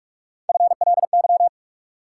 Morse code, heard as a keyed tone, about 45 wpm, 700 Hz. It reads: FLY
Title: Sentences from Top 500 Words - New Word Drill - 45wpm